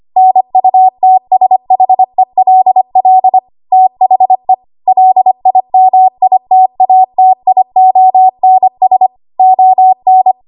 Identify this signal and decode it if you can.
Morse code, heard as a keyed tone, about 25 wpm, 750 Hz. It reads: NUTSHELL THE LIMITATIONS ON
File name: Archive / 25 WPM / 25WPM